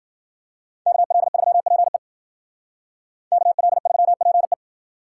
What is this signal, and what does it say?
Morse code, heard as a keyed tone, about 53 wpm, 700 Hz. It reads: KL3PE KL3PE